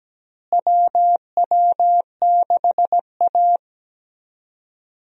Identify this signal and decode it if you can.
Morse code, heard as a keyed tone, about 17 wpm, 700 Hz. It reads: WW6A
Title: USA Call Signs - 4 Characters - Repeated 2x - 17wpm